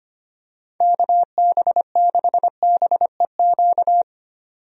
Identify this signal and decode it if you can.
Morse code, heard as a keyed tone, about 25 wpm, 700 Hz. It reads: KB6BEQ
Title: USA Call Signs - 6 Characters - 25wpm